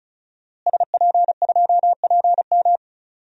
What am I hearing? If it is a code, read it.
Morse code, heard as a keyed tone, about 36 wpm, 700 Hz. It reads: SP2PM
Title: DXCC-100 Call Signs - 5 Characters - Repeated 3x - 35wpm